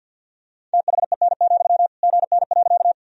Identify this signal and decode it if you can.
Morse code, heard as a keyed tone, about 46 wpm, 700 Hz. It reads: THEN, GN.
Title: Sentences from Top 300 Words and Abbreviations 50wpm